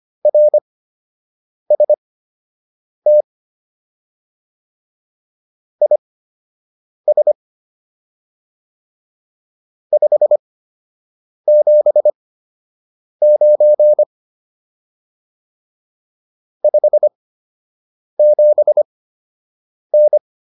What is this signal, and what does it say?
Morse code, heard as a keyed tone, about 25 wpm, spaced as for 7 wpm, 600 Hz. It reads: RST IS 579 57N